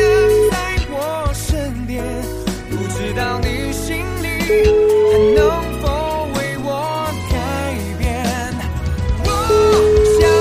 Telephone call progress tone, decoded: ringback tone

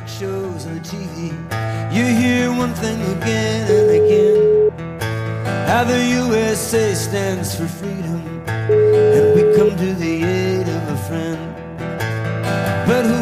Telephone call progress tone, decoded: ringback tone

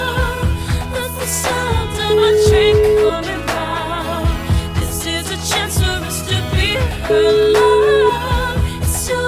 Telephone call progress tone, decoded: ringback tone

